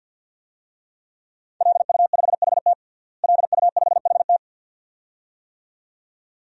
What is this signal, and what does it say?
Morse code, heard as a keyed tone, about 48 wpm, 700 Hz. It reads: LU5HT LU5HT
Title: DXCC-100 Call Signs - 5 Characters - Repeated 2x - 50wpm